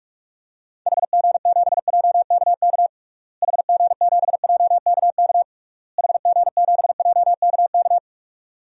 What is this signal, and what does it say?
Morse code, heard as a keyed tone, about 47 wpm, 700 Hz. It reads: HG7JKK HG7JKK HG7JKK
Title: DXCC-100 Call Signs - 6 Characters - Repeated 3x - 45wpm